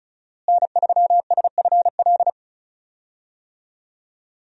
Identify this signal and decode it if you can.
Morse code, heard as a keyed tone, about 36 wpm, 700 Hz. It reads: N3SFL
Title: USA Call Signs - 5 Characters - Repeated 3x - 35wpm